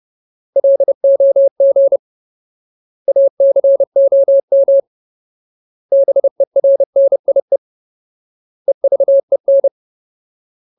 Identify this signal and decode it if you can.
Morse code, heard as a keyed tone, about 30 wpm, 550 Hz. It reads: LOG ACOM BERNIE EVEN
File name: Ultimate Word List / 30WPM Ultimate